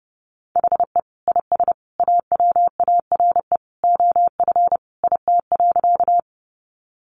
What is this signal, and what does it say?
Morse code, heard as a keyed tone, about 30 wpm, 700 Hz. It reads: HE IS AWARE OF IT.
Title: Sentences from Top 1200 Words 30wpm